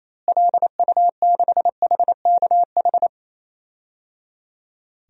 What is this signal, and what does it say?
Morse code, heard as a keyed tone, about 28 wpm, 700 Hz. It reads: LU6HKH
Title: DXCC-100 Call Signs - 6 Characters - Repeated 3x - 28wpm